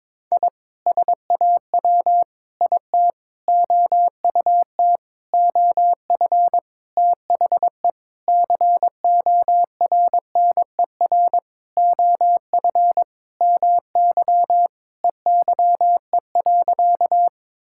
Morse code, heard as a keyed tone, about 22 wpm, 700 Hz. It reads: I SAW IT OUT OF THE CORNER OF MY EYE.
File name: Sentences with 7-Word Phrases - 22wpm